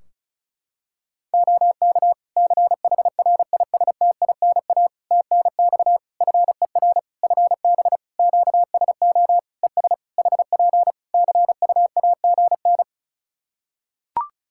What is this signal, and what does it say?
Morse code, heard as a keyed tone, about 36 wpm, 700 Hz. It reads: OK CHRISTINA TNX FER FB QSO ES HP CUAGN
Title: Practice QSO - Exchange 4 - Ending - 35wpm